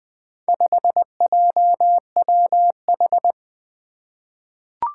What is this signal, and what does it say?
Morse code, heard as a keyed tone, about 21 wpm, 700 Hz. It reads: 5JWH
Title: Four Challenging Letter-Number Combinations 20wpm